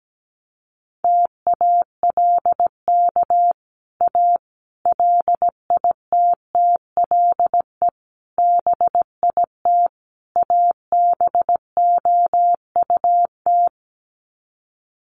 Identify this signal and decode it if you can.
Morse code, heard as a keyed tone, about 17 wpm, 700 Hz. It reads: TALK A LITTLE BIT ABOUT
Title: Sets of 5 Words Encore 17wpm